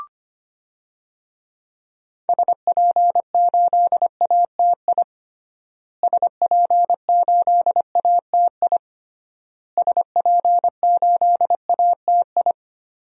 Morse code, heard as a keyed tone, about 25 wpm, 700 Hz. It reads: SP8ATI SP8ATI SP8ATI
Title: DXCC-100 Call Signs - 6 Characters - Repeated 3x - 25wpm